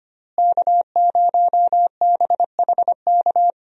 Morse code, heard as a keyed tone, about 25 wpm, 700 Hz. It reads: K0BHK
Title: USA Call Signs - 5 Characters - Repeated 3x - 25wpm